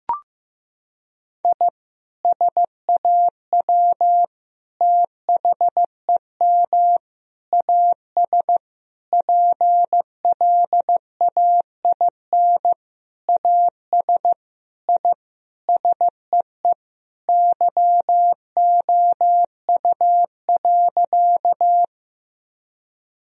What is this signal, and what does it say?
Morse code, heard as a keyed tone, about 15 wpm, 700 Hz. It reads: I SAW THEM AS PLAIN AS I SEE YOU.
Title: Sentences from Top 1000 Words - Review Part 2 - 15wpm